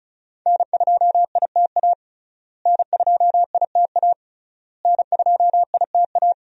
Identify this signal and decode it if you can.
Morse code, heard as a keyed tone, about 36 wpm, 700 Hz. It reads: N2ITA N2ITA N2ITA
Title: USA Call Signs - 5 Characters - Repeated 3x - 35wpm